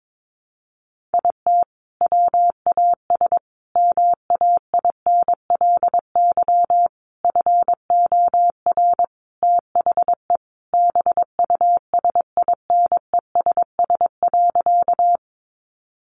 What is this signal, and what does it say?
Morse code, heard as a keyed tone, about 22 wpm, 700 Hz. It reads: IT WAS MAINLY FOR THE BUSINESS.
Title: Sentences from Top 1400 Words 22wpm